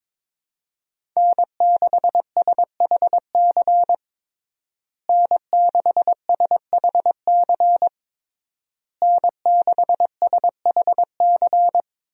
Morse code, heard as a keyed tone, about 22 wpm, 700 Hz. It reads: N6SHC N6SHC N6SHC
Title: USA Call Signs - 5 Characters - Repeated 3x - 22wpm